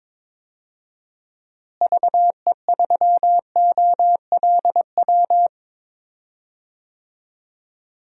Morse code, heard as a keyed tone, about 21 wpm, 700 Hz. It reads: VE3OLW
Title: DXCC-100 Call Signs - 6 Characters - Repeated 2x - 22wpm